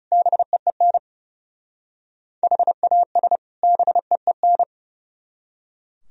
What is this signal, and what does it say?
Morse code, heard as a keyed tone, about 36 wpm, 700 Hz. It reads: BEEN HAS BEEN
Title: Speed-Racing - Sets of 2 Words 30wpm